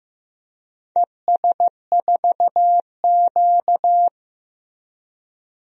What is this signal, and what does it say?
Morse code, heard as a keyed tone, about 15 wpm, 700 Hz. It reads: ES4Q
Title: DXCC-100 Call Signs - 4 Characters - Repeated 2x - 15wpm